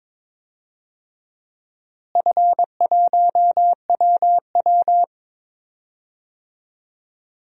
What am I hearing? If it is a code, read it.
Morse code, heard as a keyed tone, about 22 wpm, 700 Hz. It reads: F1WW
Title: DXCC-100 Call Signs - 4 Characters - Repeated 2x - 22wpm